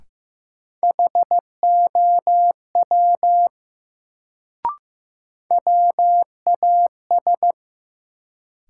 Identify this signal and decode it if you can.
Morse code, heard as a keyed tone, about 15 wpm, 700 Hz. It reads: HOW WAS
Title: Sentences from Top 100 Words - New Word Drill - 15wpm